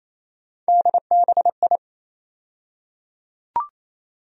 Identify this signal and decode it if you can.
Morse code, heard as a keyed tone, about 28 wpm, 700 Hz. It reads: DBI
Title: USA Call Sign Postfixes - 3 Letters 28wpm